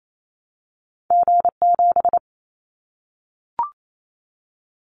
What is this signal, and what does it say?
Morse code, heard as a keyed tone, about 28 wpm, 700 Hz. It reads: G7
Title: Callsign Prefixes for Top 25 DXCC - 28wpm